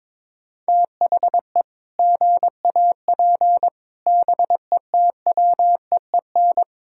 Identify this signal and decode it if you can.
Morse code, heard as a keyed tone, about 22 wpm, 700 Hz. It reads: THE GAP BETWEEN
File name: practice / 22wpm / Sets of 8 Words - 22wpm